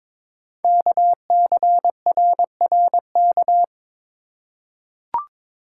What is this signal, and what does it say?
Morse code, heard as a keyed tone, about 22 wpm, 700 Hz. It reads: KCRRK
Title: Code Groups - 5 Letter:Numbers - 22wpm